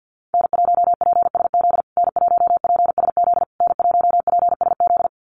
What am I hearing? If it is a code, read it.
Morse code, heard as a keyed tone, about 50 wpm, 700 Hz. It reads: N1PSZ N1PSZ N1PSZ
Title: USA Call Signs Repeated 3x 50wpm